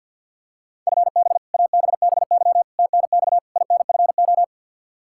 Morse code, heard as a keyed tone, about 47 wpm, 700 Hz. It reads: FB ABBY TNX INFO
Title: Practice QSO - Exchange 3 - More Info - 50wpm